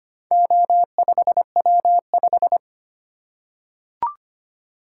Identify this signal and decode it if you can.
Morse code, heard as a keyed tone, about 25 wpm, 700 Hz. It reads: O5W5
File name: practice / 25wpm / Four Letter-Number Combinations - 25wpm